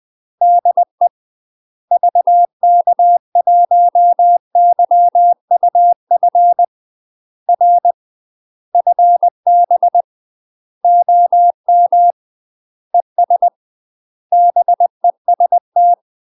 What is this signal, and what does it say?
Morse code, heard as a keyed tone, about 20 wpm, 700 Hz. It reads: DE VK1YUF R FB OM ES BEST
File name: UK MASTER QSOs TRAINING / 20WPM / UK MASTER QSOs 0059